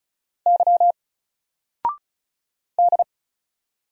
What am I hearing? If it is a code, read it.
Morse code, heard as a keyed tone, about 35 wpm, 700 Hz. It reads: Y D
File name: Single-Letter-Number-35wpm